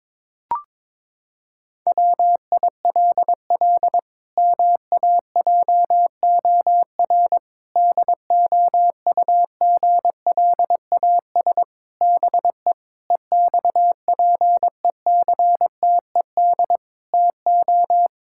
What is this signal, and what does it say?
Morse code, heard as a keyed tone, about 22 wpm, 700 Hz. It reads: WILL MAJOR DOUGLAS BE EXPECTED TO
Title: Pangram Challenge v2 - 22wpm